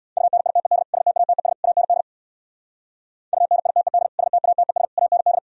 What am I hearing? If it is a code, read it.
Morse code, heard as a keyed tone, about 62 wpm, 700 Hz. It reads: UNITED STATES AND UNITED STATES AND
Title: Speed-Racing - Sets of 3 Words 50wpm